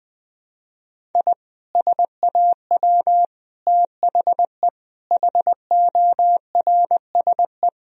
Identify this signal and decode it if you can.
Morse code, heard as a keyed tone, about 20 wpm, 700 Hz. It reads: I SAW THE HORSE
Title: Sentences from Top 1000 Words - Review Part 2 - 20wpm